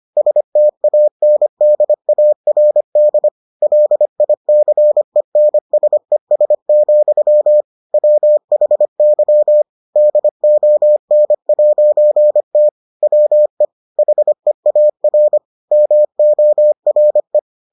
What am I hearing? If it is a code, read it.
Morse code, heard as a keyed tone, about 25 wpm, 600 Hz. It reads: STANDARD LICENSES, WHY DON'T WE HEAR MORE